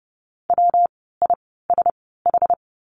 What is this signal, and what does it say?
Morse code, heard as a keyed tone, about 30 wpm, spaced as for 17 wpm, 700 Hz. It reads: WISH
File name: course / 30:10wpm / 115 Lesson 19f - Letter w - Words 30wpm